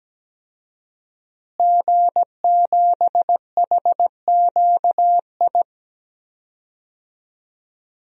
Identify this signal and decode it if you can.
Morse code, heard as a keyed tone, about 17 wpm, 700 Hz. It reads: G7HQI
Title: DXCC-100 Call Signs - 5 Characters - Repeated 2x - 17wpm